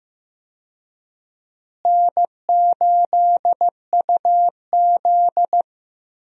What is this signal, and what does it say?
Morse code, heard as a keyed tone, about 15 wpm, 700 Hz. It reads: N8UZ
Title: USA Call Signs - 4 Characters - Repeated 3x - 15wpm